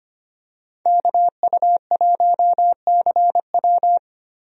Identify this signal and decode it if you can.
Morse code, heard as a keyed tone, about 25 wpm, 700 Hz. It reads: KU1CW
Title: USA Call Signs - 5 Characters - Repeated 3x - 25wpm